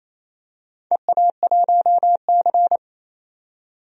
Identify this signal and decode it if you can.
Morse code, heard as a keyed tone, about 27 wpm, 700 Hz. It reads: EA1C
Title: DXCC-100 Call Signs - 4 Characters - 28wpm